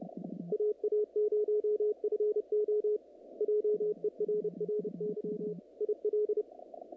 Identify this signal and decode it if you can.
Morse code, heard as a keyed tone, about 30 wpm, 440 Hz, with radio noise. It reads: AA0FO JERRY IL
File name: contests / naqp-cw-k5tr-op-k5pi- -2200